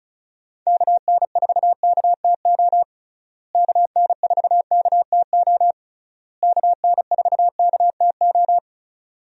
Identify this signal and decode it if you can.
Morse code, heard as a keyed tone, about 34 wpm, 700 Hz. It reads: KN4KTO KN4KTO KN4KTO
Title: USA Call Signs - 6 Characters - Repeated 3x - 35wpm